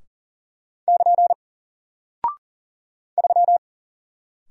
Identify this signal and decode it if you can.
Morse code, heard as a keyed tone, about 40 wpm, 700 Hz. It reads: <KN> 3
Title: Single Character (Letter:Number:Punctuation:Prosign) Alt 40wpm